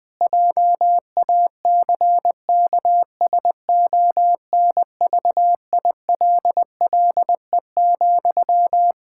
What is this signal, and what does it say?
Morse code, heard as a keyed tone, about 20 wpm, 700 Hz. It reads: JACKSONVILLE,
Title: Practice QSO - Exchange 1 - Beginning - 20wpm